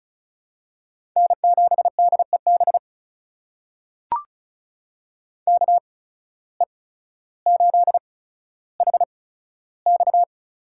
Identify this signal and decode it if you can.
Morse code, heard as a keyed tone, about 34 wpm, spaced as for 15 wpm, 700 Hz. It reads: N7DEB KE8HX
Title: USA Call Signs - 8x Character Spacing - 35wpm